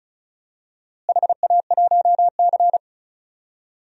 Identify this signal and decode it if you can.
Morse code, heard as a keyed tone, about 36 wpm, 700 Hz. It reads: HA1C
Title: DXCC-100 Call Signs - 4 Characters - Repeated 3x - 35wpm